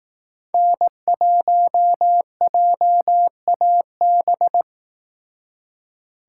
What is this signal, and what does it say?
Morse code, heard as a keyed tone, about 18 wpm, 700 Hz. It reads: N1JAB
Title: Speed Racing - USA Call Signs 15wpm